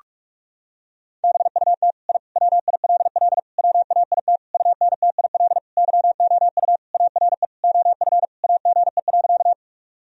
Morse code, heard as a keyed tone, about 46 wpm, 700 Hz. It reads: BUT I WILL WAIT UNTIL YOU ARE OF AGE.
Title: Sentences from Top 1000 Words - Review Part 3 - 45wpm